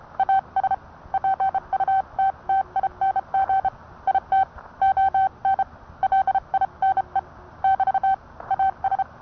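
Morse code, heard as a keyed tone, about 30 wpm, 770 Hz. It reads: AS PUTTING IT ON LINE <BT> AS
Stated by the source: Morse code sent by N1EA on a Vibroplex key.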